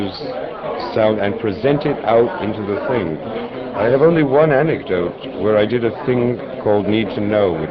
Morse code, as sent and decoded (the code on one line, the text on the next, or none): none